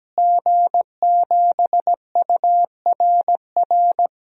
Morse code, heard as a keyed tone, about 17 wpm, 700 Hz. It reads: G7URR
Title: DXCC-100 Call Signs - 5 Characters - Repeated 3x - 17wpm